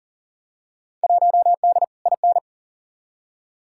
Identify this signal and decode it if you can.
Morse code, heard as a keyed tone, about 40 wpm, 700 Hz. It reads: 1D IN
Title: Field Day Exchanges - 40wpm